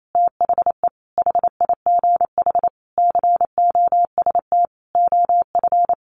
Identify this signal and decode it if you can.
Morse code, heard as a keyed tone, about 28 wpm, 700 Hz. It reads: THE HIGH COST OF
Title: Sentences from Top 1400 Words 28wpm